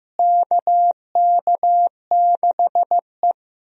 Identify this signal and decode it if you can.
Morse code, heard as a keyed tone, about 14 wpm, 700 Hz. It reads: KK6E
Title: USA Call Signs - 4 Characters - Repeated 3x - 15wpm